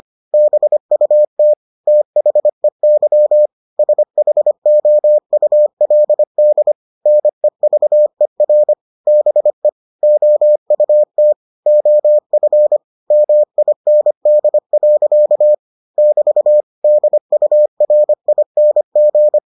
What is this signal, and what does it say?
Morse code, heard as a keyed tone, about 25 wpm, 600 Hz. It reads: BUT THEY SHOULD NEVER BE OUT OF MIND. <BT> DURING